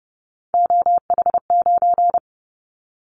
Morse code, heard as a keyed tone, about 30 wpm, 700 Hz. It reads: OH9
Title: Callsign Prefixes for Top 50 DXCC - 30wpm